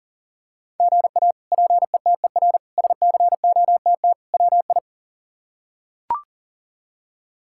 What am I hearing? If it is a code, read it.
Morse code, heard as a keyed tone, about 40 wpm, 700 Hz. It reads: GA PETER SCOTT WI
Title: K1USN SST QSO Practice - 40wpm